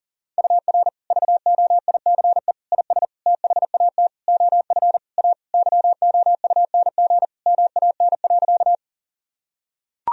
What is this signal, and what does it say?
Morse code, heard as a keyed tone, about 37 wpm, 700 Hz. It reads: UR VOICE IS THAT OF A YOUNG MAN.
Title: Sentences from Top 500 Words and Abbreviations 40wpm